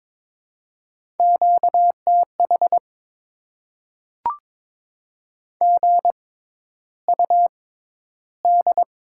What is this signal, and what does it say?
Morse code, heard as a keyed tone, about 22 wpm, spaced as for 11 wpm, 690 Hz. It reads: QTH GUD
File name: Top 20 QSO Elements - 6x Character Spacing - 22wpm